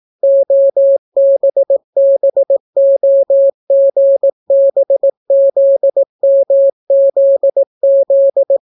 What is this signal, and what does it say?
Morse code, heard as a keyed tone, about 18 wpm, 550 Hz. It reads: OBBOGBZMZZ